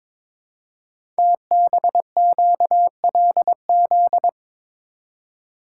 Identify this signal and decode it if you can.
Morse code, heard as a keyed tone, about 22 wpm, 700 Hz. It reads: TBQLZ
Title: Code Groups - 5 Letter:Numbers - 22wpm